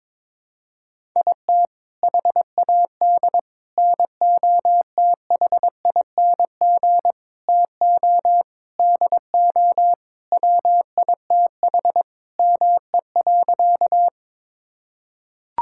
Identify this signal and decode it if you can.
Morse code, heard as a keyed tone, about 22 wpm, 700 Hz. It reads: IT HAD NOTHING TO DO WITH ME.
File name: practice / 22wpm / Sentences with 4-Word Phrases - 22wpm